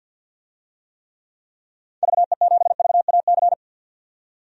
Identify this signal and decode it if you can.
Morse code, heard as a keyed tone, about 46 wpm, 700 Hz. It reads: VE7VAC
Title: DXCC-100 Call Signs - 6 Characters - 50wpm